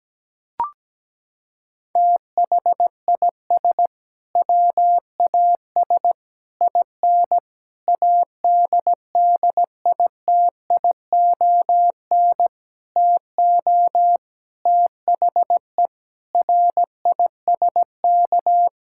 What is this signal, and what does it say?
Morse code, heard as a keyed tone, about 17 wpm, 700 Hz. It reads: THIS WAS IN ADDITION TO THE RISK
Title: Sentences from Top 1000 Words - Review Part 6 - 17wpm